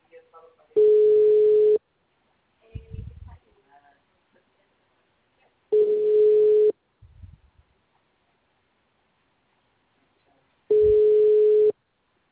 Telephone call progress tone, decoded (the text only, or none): ringback tone